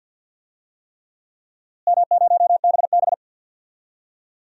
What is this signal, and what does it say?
Morse code, heard as a keyed tone, about 47 wpm, 700 Hz. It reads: M0BB
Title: DXCC-100 Call Signs - 4 Characters - Repeated 3x - 50wpm